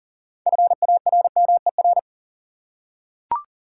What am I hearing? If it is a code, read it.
Morse code, heard as a keyed tone, about 40 wpm, 700 Hz. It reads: FARMER
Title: Six Letter Words - 40wpm